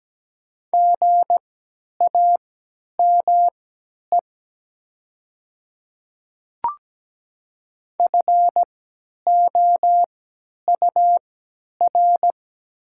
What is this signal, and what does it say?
Morse code, heard as a keyed tone, about 17 wpm, spaced as for 10 wpm, 690 Hz. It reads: GAME FOUR
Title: 3x Character Spacing - Top 500 Words - 17wpm